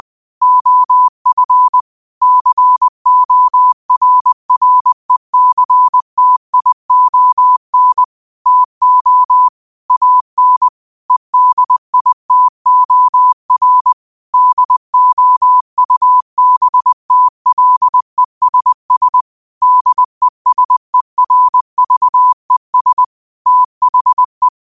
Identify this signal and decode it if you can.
Morse code, heard as a keyed tone, about 20 wpm, 1000 Hz. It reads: OF CORRECTION TO AN EDITOR DOUBTLESS DESERVES THE
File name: qotdfiles / 20wpm